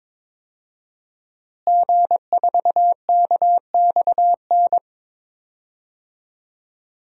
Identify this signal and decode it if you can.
Morse code, heard as a keyed tone, about 22 wpm, 700 Hz. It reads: G4KXN
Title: DXCC-100 Call Signs - 5 Characters - Repeated 2x - 22wpm